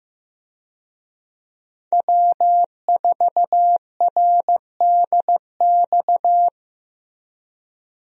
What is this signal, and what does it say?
Morse code, heard as a keyed tone, about 15 wpm, 700 Hz. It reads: W4RDX
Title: USA Call Signs - 5 Characters - Repeated 2x - 15wpm